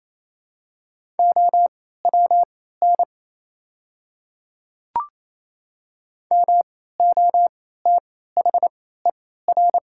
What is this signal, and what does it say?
Morse code, heard as a keyed tone, about 27 wpm, spaced as for 16 wpm, 700 Hz. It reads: OWN MOTHER
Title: 3x Character Spacing - Top 300 Words - 28wpm